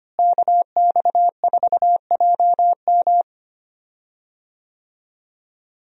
Morse code, heard as a keyed tone, about 25 wpm, 700 Hz. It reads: KX4JM
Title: USA Call Signs - 5 Characters - Repeated 3x - 25wpm